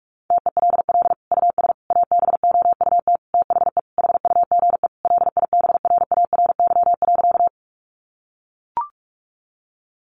Morse code, heard as a keyed tone, about 44 wpm, 700 Hz. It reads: TELL US ABOUT THE HUGE LIBRARY.
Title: Sentences from Top 1100 Words 45wpm